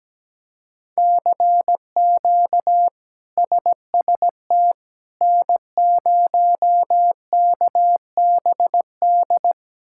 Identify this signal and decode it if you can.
Morse code, heard as a keyed tone, about 17 wpm, 700 Hz. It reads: CQ SST N0KBD
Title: K1USN SST QSO Practice - 17wpm